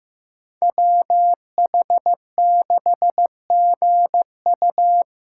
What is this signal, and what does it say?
Morse code, heard as a keyed tone, about 15 wpm, 700 Hz. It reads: WH6GU
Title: USA Call Signs - 5 Characters - 6x Character Spacing - 15wpm